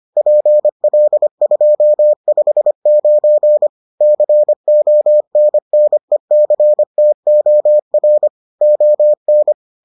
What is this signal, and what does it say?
Morse code, heard as a keyed tone, about 25 wpm, 600 Hz. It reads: PL259 CONNECTOR ON